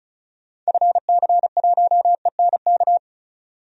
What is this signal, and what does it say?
Morse code, heard as a keyed tone, about 36 wpm, 700 Hz. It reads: FC1ENK
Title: DXCC-100 Call Signs - 6 Characters - 35wpm